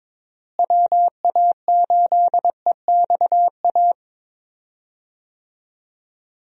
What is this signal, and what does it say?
Morse code, heard as a keyed tone, about 22 wpm, 700 Hz. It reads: WA8EXA
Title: USA Call Signs - 6 Characters - Repeated 2x - 22wpm